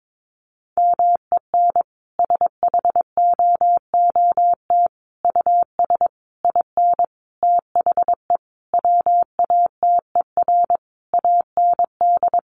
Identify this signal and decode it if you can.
Morse code, heard as a keyed tone, about 22 wpm, 700 Hz. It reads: MEN SHOOT US IN THE WATER AND
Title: Sentences from Top 1200 Words 22wpm